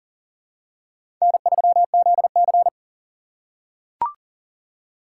Morse code, heard as a keyed tone, about 42 wpm, 690 Hz. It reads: N3ZC
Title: USA Call Signs - 4 Characters - 2x Character Spacing - 40wpm